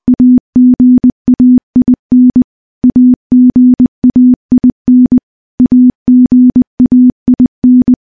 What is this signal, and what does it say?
Morse code, heard as a keyed tone, about 20 wpm, 250 Hz. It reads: AGAIN AGAIN AGAIN
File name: morse / morse-training-20WPM